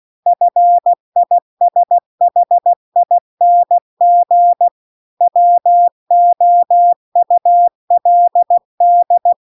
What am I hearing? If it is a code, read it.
Morse code, heard as a keyed tone, about 16 wpm, 700 Hz. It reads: FISHING WOULD
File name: CW-The-Big-Twoheaded-River-2-16WPM